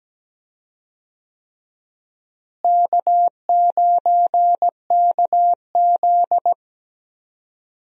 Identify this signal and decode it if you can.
Morse code, heard as a keyed tone, about 17 wpm, 700 Hz. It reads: K9KZ
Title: USA Call Signs - 4 Characters - Repeated 3x - 17wpm